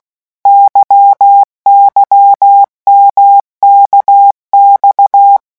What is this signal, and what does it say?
Morse code, heard as a keyed tone, about 16 wpm, 800 Hz. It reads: YYMKX
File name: morse mp3 / revision1-vitesse 15 mots